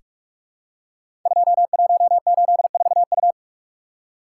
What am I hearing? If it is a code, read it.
Morse code, heard as a keyed tone, about 44 wpm, 700 Hz. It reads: 2184U
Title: Code Groups - 5 Letter:Numbers - 45wpm